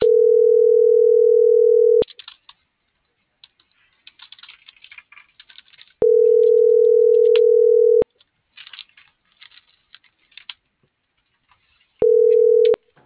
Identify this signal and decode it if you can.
Ringback tone (North America), 440 + 480 Hz.